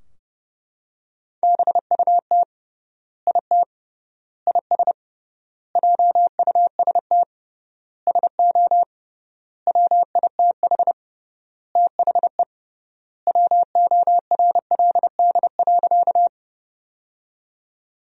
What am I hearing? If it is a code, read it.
Morse code, heard as a keyed tone, about 30 wpm, 700 Hz. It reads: BUT IT IS JUST SO WITH THE WORLD.
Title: 3x Word-Spacing - Sentences from Top 200 Words - 30wpm